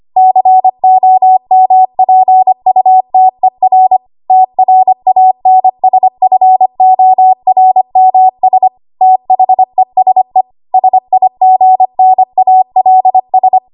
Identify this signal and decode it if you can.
Morse code, heard as a keyed tone, about 25 wpm, 750 Hz. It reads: COMPUTER TRANSFORMS THESE SIGNALS
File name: Morse / 25WPM